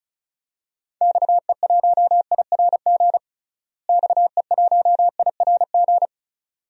Morse code, heard as a keyed tone, about 36 wpm, 700 Hz. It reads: XE1IRG XE1IRG
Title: DXCC-100 Call Signs - 6 Characters - Repeated 2x - 35wpm